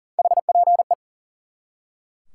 Morse code, heard as a keyed tone, about 40 wpm, 700 Hz. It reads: HPE
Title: Top 180 QSO Elements 40wpm